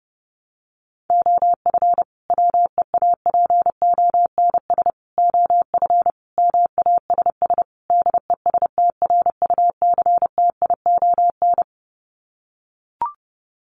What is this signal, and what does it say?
Morse code, heard as a keyed tone, about 30 wpm, 700 Hz. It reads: OF WEAPONS OF MASS DESTRUCTION
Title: Sets of 5 Words Encore 30wpm